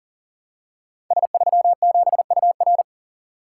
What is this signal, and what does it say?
Morse code, heard as a keyed tone, about 42 wpm, 700 Hz. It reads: S37UR
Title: Code Groups - 5 Letter:Numbers - 40wpm